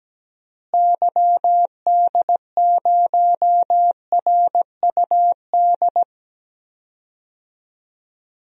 Morse code, heard as a keyed tone, about 17 wpm, 700 Hz. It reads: YD0RUD
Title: DXCC-100 Call Signs - 6 Characters - Repeated 2x - 17wpm